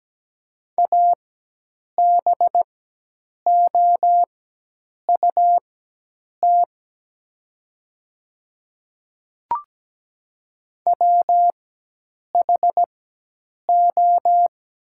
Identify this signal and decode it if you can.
Morse code, heard as a keyed tone, about 17 wpm, spaced as for 8 wpm, 700 Hz. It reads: ABOUT WHO
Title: 4x Character Spacing - Top 100 Words - 17wpm